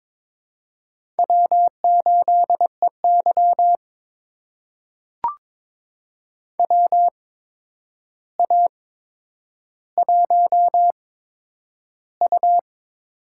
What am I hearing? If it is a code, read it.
Morse code, heard as a keyed tone, about 23 wpm, spaced as for 10 wpm, 700 Hz. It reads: W8EY WA1U
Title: USA Call Signs - 4 Characters - 8x Character Spacing - 22wpm